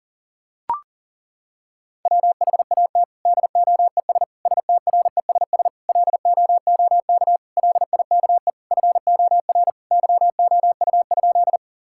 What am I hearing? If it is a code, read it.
Morse code, heard as a keyed tone, about 38 wpm, 700 Hz. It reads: WHAT DOES STRESS LOOK LIKE FOR YOU?
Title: Sentences from Top 1000 Words - Review Part 6 - 40wpm